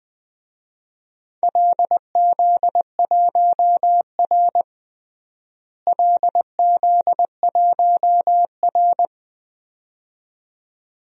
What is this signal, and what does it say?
Morse code, heard as a keyed tone, about 20 wpm, 700 Hz. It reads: LZ1R LZ1R
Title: DXCC-100 Call Signs - 4 Characters - Repeated 2x - 20wpm